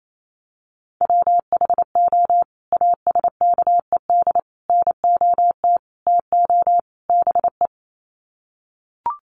Morse code, heard as a keyed tone, about 28 wpm, 700 Hz. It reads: WHO ASKED NOT TO BE
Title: Sets of 5 Words Encore2 28wpm